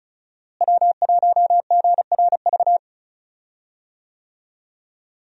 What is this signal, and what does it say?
Morse code, heard as a keyed tone, about 35 wpm, 700 Hz. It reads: W1GRV
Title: USA Call Signs - 5 Characters - Repeated 2x - 35wpm